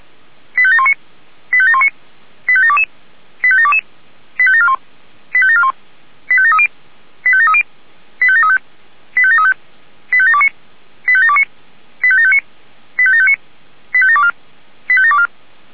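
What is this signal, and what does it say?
Five-tone selective call (ZVEI), heard as quick runs of five tones, tones of about 70 ms. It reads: 85719 85719 85722 85722 85741 85741 85730 85730 85735 85735 85729 85729 85759 85759 85724 85724
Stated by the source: ZVEI five-tone selective call received via SDR.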